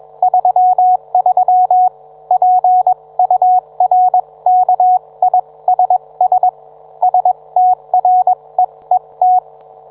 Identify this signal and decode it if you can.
Morse code, heard as a keyed tone, about 22 wpm, spaced as for 19 wpm, 730 Hz. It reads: 33 PURKISS STREET